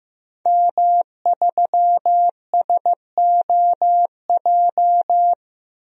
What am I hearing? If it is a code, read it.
Morse code, heard as a keyed tone, about 15 wpm, 700 Hz. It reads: M3SOJ
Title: DXCC-100 Call Signs - 5 Characters - 15wpm